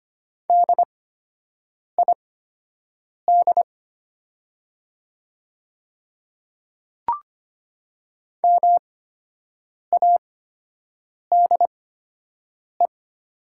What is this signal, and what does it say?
Morse code, heard as a keyed tone, about 25 wpm, spaced as for 7 wpm, 700 Hz. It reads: DID MADE
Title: 8x Character Spacing - Top 300 Words - 25wpm